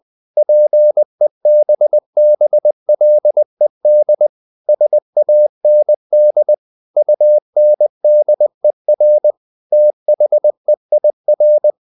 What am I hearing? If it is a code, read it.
Morse code, heard as a keyed tone, about 20 wpm, 600 Hz. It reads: PEBBLED SAND UNDER THEIR
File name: the old man and the sea / the old man and the sea part 1 20wpm